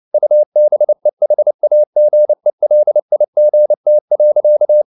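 Morse code, heard as a keyed tone, about 29 wpm, 600 Hz. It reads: UBEHAGELIGT.
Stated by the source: Fast (QRQ) Morse at 29 wpm, from CW med Gnister.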